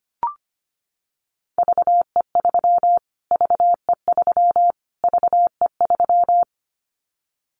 Morse code, heard as a keyed tone, about 25 wpm, 700 Hz. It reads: VE3 VE3 VE3
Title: Callsign Prefixes for Top 25 DXCC - Repeated 3x - 25wpm